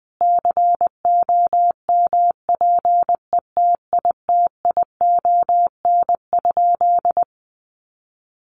Morse code, heard as a keyed tone, about 20 wpm, 700 Hz. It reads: COMPETITION?
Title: Sentences from Top 1000 Words 20wpm